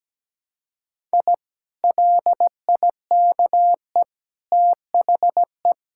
Morse code, heard as a keyed tone, about 17 wpm, 700 Hz. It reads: I LIKE THE
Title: Sentences from Top 1000 Words - Review Part 5 - 17wpm